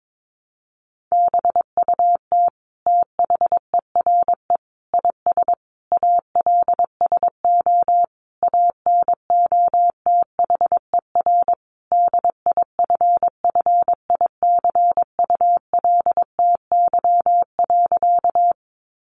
Morse code, heard as a keyed tone, about 22 wpm, 700 Hz. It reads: BUT THERE IS ALSO ANOTHER DIFFICULTY.
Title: Sentences from Top 1500 Words 22wpm